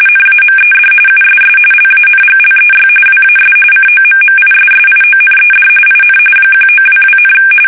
RTTY, about 45.45 baud. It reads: NAMED USS LANGLEY WAS COMMISSIONED IN 1922,
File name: fsk850